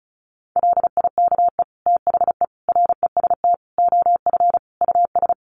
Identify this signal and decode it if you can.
Morse code, heard as a keyed tone, about 34 wpm, 700 Hz. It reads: LIKE THE REST OF US
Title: Sets of 5 Words Encore2 35wpm